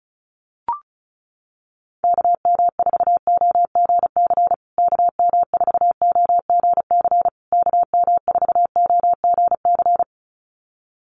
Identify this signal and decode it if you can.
Morse code, heard as a keyed tone, about 34 wpm, 700 Hz. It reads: KM4OGC KM4OGC KM4OGC
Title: USA Call Signs Repeated 3x 35wpm